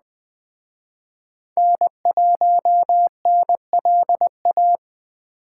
Morse code, heard as a keyed tone, about 20 wpm, 700 Hz. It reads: N1NLA
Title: Speed Racing - USA 1x3 Call Signs 17wpm